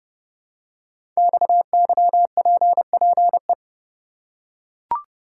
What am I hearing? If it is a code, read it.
Morse code, heard as a keyed tone, about 30 wpm, 700 Hz. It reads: XYPPE
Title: Code Groups - 5 Letters - 30wpm